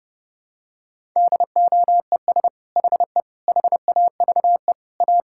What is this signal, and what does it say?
Morse code, heard as a keyed tone, about 32 wpm, 700 Hz. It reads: DOES HE HAVE A
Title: Sentences from Top 1000 Words - Review Part 6 - 30wpm